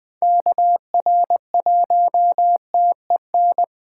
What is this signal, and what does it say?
Morse code, heard as a keyed tone, about 20 wpm, 700 Hz. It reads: KR1TEN
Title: Speed Racing - USA 2x3 Call Signs 17wpm